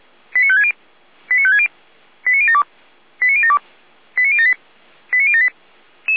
Five-tone selective call (ZVEI), heard as quick runs of five tones, tones of about 70 ms. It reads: 89560 89560 89072 89072 89078 89078
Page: audio.f85360.feuerwehr-orlamuende.org